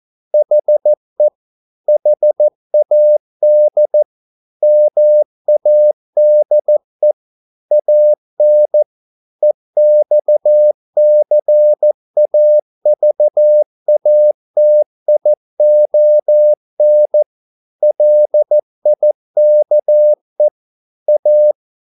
Morse code, heard as a keyed tone, about 14 wpm, 600 Hz. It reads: HE HAD MADE AN EXCAVATION LIKE A
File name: mp3 files / a dogs mistake - 14wpm